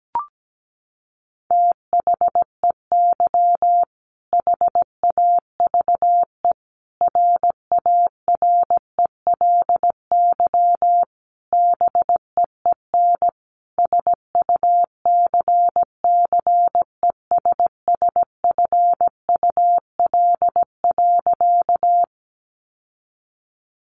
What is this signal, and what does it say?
Morse code, heard as a keyed tone, about 17 wpm, 700 Hz. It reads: THEY HAVE RARELY BEEN SUCCESSFUL.
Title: Sentences from Top 1400 Words 17wpm